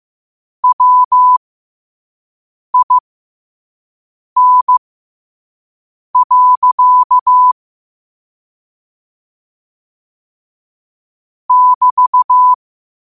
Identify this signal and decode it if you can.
Morse code, heard as a keyed tone, about 15 wpm, spaced as for 5 wpm, 1000 Hz. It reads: WIN. <BT>
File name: 5wpm